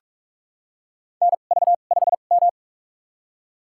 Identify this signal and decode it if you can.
Morse code, heard as a keyed tone, about 46 wpm, spaced as for 33 wpm, 700 Hz. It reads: NV5M
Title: USA Call Signs - 4 Characters - 2x Character Spacing - 45wpm